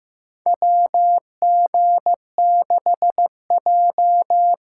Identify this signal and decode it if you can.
Morse code, heard as a keyed tone, about 15 wpm, 700 Hz. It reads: WG6J
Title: USA Call Signs - 4 Characters - 2x Character Spacing - 15wpm